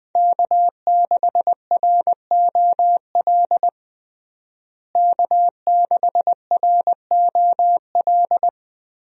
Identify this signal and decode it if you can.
Morse code, heard as a keyed tone, about 20 wpm, 700 Hz. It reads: K6ROL K6ROL
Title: USA Call Signs - 5 Characters - Repeated 2x - 20wpm